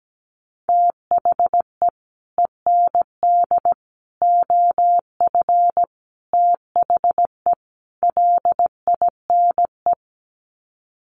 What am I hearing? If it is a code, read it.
Morse code, heard as a keyed tone, about 17 wpm, 700 Hz. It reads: THE END OF THE LINE
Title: Sets of 5 Words Encore2 17wpm